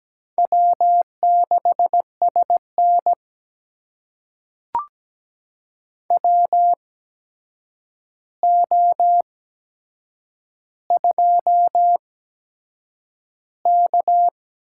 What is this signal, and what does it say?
Morse code, heard as a keyed tone, about 17 wpm, spaced as for 7 wpm, 700 Hz. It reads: W6SN WO2K
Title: USA Call Signs - 4 Characters - 8x Character Spacing - 17wpm